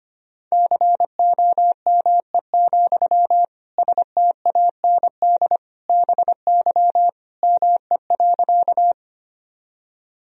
Morse code, heard as a keyed tone, about 25 wpm, 700 Hz. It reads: COME, STAND BY ME.
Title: Sentences from Top 1000 Words - Review Part 1 - 25wpm